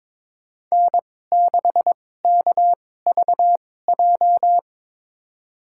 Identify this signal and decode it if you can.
Morse code, heard as a keyed tone, about 21 wpm, spaced as for 16 wpm, 700 Hz. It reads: N6KVJ
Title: USA Call Signs - 5 Characters - 2x Character Spacing - 22wpm